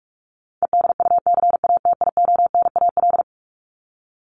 Morse code, heard as a keyed tone, about 44 wpm, 700 Hz. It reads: EDUCATIONAL
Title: Top 1001-1250 Words 45wpm